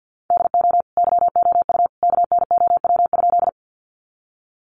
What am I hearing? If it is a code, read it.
Morse code, heard as a keyed tone, about 50 wpm, 700 Hz. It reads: DO YOU KNOW?
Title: Sentences from Top 100 Words 50wpm